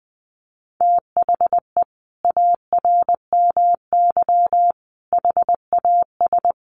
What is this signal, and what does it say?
Morse code, heard as a keyed tone, about 19 wpm, 700 Hz. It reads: THE ARMY HAS
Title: Sentences from Top 1300 Words 20wpm